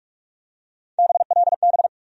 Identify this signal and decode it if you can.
Morse code, heard as a keyed tone, about 42 wpm, 700 Hz. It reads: BLB
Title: USA Call Sign Postfixes - 3 Letters 45wpm